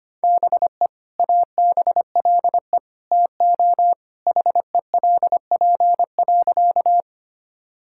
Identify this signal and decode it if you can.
Morse code, heard as a keyed tone, about 25 wpm, 700 Hz. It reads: BE ABLE TO HELP.
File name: Sentences with 2-Word Phrases - 25wpm